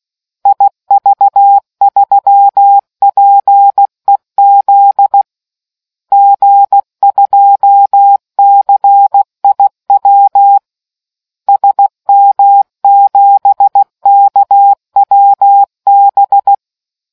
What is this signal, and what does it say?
Morse code, heard as a keyed tone, about 16 wpm, 800 Hz. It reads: IV3PEZ G2CIW SM7KWB